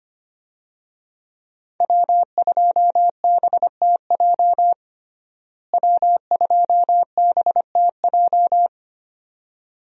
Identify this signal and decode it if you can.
Morse code, heard as a keyed tone, about 25 wpm, 700 Hz. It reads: W2BTJ W2BTJ
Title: USA Call Signs - 5 Characters - Repeated 2x - 25wpm